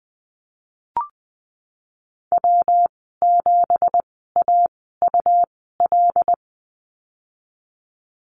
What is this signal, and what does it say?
Morse code, heard as a keyed tone, about 19 wpm, spaced as for 14 wpm, 700 Hz. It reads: W7AUL
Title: 179 Lesson 29g - Slash - Call Signs 20wpm